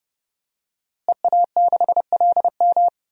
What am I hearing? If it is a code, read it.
Morse code, heard as a keyed tone, about 31 wpm, 700 Hz. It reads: EA6LM
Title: DXCC-100 Call Signs - 5 Characters - Repeated 3x - 30wpm